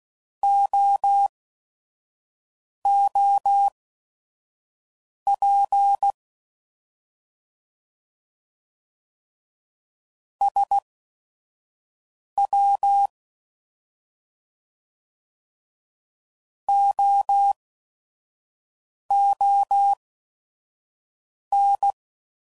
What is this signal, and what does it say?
Morse code, heard as a keyed tone, about 16 wpm, spaced as for 4 wpm, 800 Hz. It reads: OOP SWOON